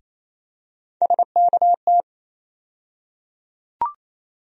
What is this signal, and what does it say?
Morse code, heard as a keyed tone, about 28 wpm, 700 Hz. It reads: SKT
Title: USA Call Sign Postfixes - 3 Letters 28wpm